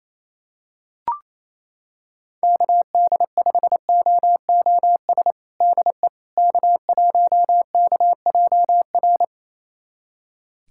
Morse code, heard as a keyed tone, about 28 wpm, 700 Hz. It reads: KD5OOS DE K1KJR
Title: USA Callsign DE Callsign 28wpm